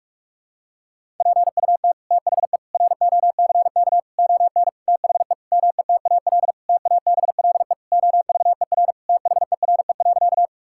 Morse code, heard as a keyed tone, about 45 wpm, 700 Hz. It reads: PUT THE ROCK ON THE METAL TABLE OVER THERE.
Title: Sentences from Top 1000 Words - Review Part 6 - 45wpm